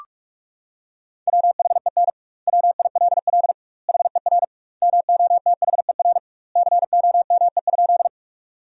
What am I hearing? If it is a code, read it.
Morse code, heard as a keyed tone, about 41 wpm, 700 Hz. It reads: WHEN WILL HER MOTHER COME?
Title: 2x Word-Spacing - Sentences from Top 200 Words - 45wpm